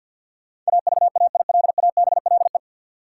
Morse code, heard as a keyed tone, about 46 wpm, 700 Hz. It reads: AVAILABLE
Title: Sentences from Top 600 Words - New Word Drill - 50wpm